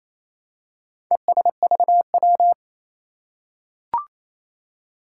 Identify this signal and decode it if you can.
Morse code, heard as a keyed tone, about 27 wpm, 700 Hz. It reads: ESVW
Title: Four Challenging Letter-Number Combinations 28wpm